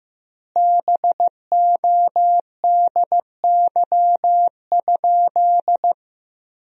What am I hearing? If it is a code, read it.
Morse code, heard as a keyed tone, about 15 wpm, 700 Hz. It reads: BODY?
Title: 2x Word-Spacing - Sentences from Top 400 Words - 15wpm